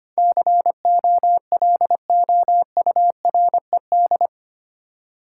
Morse code, heard as a keyed tone, about 25 wpm, 700 Hz. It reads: COLOURED
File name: Eight Letter Words - 25wpm